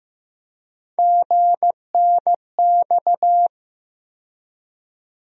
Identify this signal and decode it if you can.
Morse code, heard as a keyed tone, about 15 wpm, 700 Hz. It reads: GNX
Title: USA Call Sign Postfixes - 3 Letters 15wpm